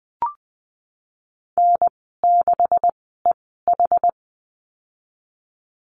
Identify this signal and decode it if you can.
Morse code, heard as a keyed tone, about 20 wpm, spaced as for 14 wpm, 700 Hz. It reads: N6EH
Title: 186 Lesson 30g - Letter b - Call Signs 20wpm